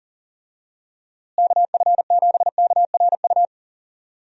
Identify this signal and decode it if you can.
Morse code, heard as a keyed tone, about 37 wpm, 700 Hz. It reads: KF7KRU
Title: USA Call Signs - 6 Characters - Repeated 3x - 40wpm